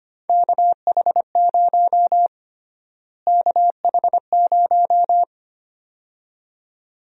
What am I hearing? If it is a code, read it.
Morse code, heard as a keyed tone, about 25 wpm, 700 Hz. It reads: KH0 KH0
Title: USA Call Sign Prefixes - Repeated 2x - 25wpm